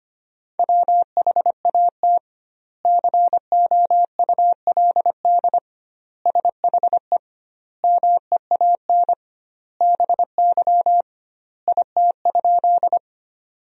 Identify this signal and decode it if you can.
Morse code, heard as a keyed tone, about 25 wpm, 700 Hz. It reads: WHAT COULD SHE MEAN BY IT?
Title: 2x Word-Spacing - Sentences from Top 200 Words - 25wpm